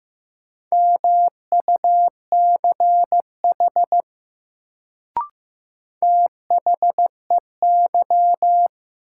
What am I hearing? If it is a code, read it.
Morse code, heard as a keyed tone, about 15 wpm, 700 Hz. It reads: MUCH THEY
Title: Sets of 2 Words - New Word Drill - 15wpm